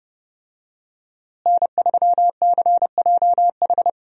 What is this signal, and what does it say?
Morse code, heard as a keyed tone, about 30 wpm, 700 Hz. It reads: N3CJH
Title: USA Call Signs - 5 Characters - Repeated 2x - 30wpm